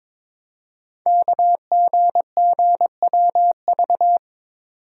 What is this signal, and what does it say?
Morse code, heard as a keyed tone, about 22 wpm, 700 Hz. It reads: KGGWV
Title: Code Groups - 5 Letters - 22wpm